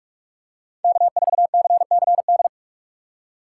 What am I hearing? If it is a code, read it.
Morse code, heard as a keyed tone, about 42 wpm, 700 Hz. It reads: K4CCD
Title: USA Call Signs - 4x Character Spacing - 45wpm